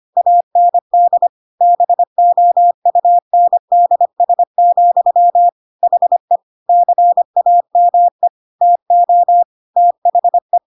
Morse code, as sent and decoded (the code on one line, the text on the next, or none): .- -. -.. / -... --- ..- -. -.. ... --..-- / .... / -.-. .- -- . / - --- / - .... .
AND BOUNDS, HE CAME TO THE